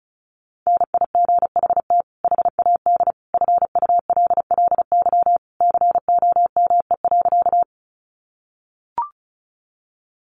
Morse code, heard as a keyed tone, about 34 wpm, 700 Hz. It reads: NIGHT HAD FULLY COME.
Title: Sentences from Top 1300 Words 35wpm